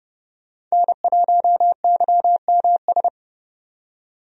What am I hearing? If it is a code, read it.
Morse code, heard as a keyed tone, about 30 wpm, 700 Hz. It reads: N1YMS
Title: USA Call Signs - 5 Characters - 30wpm